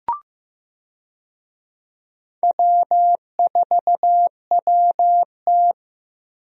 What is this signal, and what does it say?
Morse code, heard as a keyed tone, about 15 wpm, 700 Hz. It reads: W4WT
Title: USA Call Signs - 4 Characters - 15wpm